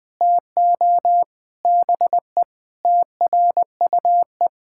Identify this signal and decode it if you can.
Morse code, heard as a keyed tone, about 20 wpm, 700 Hz. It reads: TO BE TRUE
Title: Sentences from Top 1000 Words - Review Part 6 - 20wpm